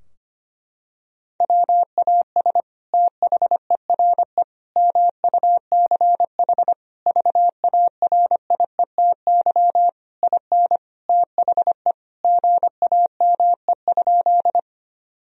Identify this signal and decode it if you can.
Morse code, heard as a keyed tone, about 25 wpm, 700 Hz. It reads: WAS THERE MUCH VARIETY IN THE GAME?
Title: Sentences from Top 1000 Words - Review Part 4 - 25wpm